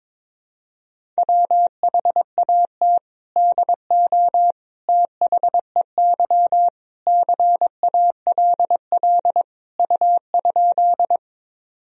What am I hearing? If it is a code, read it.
Morse code, heard as a keyed tone, about 22 wpm, 700 Hz. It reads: WHAT DO THEY CALL U?
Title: Sentences from Top 100 Words and Abbreviations 22wpm